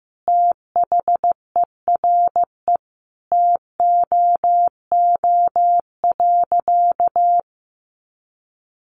Morse code, heard as a keyed tone, about 15 wpm, 700 Hz. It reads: THERE TOO.
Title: Sentences from Top 200 Words 15wpm